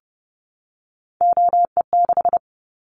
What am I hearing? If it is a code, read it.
Morse code, heard as a keyed tone, about 30 wpm, 700 Hz. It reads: OE6
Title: Callsign Prefixes for Top 50 DXCC - 30wpm